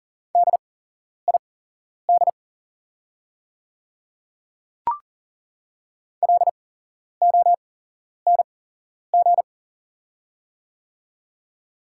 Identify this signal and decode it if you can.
Morse code, heard as a keyed tone, about 40 wpm, spaced as for 11 wpm, 700 Hz. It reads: DID LONG
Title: 8x Character Spacing - Top 100 Words - 40wpm